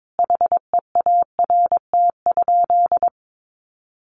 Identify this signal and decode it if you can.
Morse code, heard as a keyed tone, about 22 wpm, 700 Hz. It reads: HEART?
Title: Sentences from Top 1200 Words 22wpm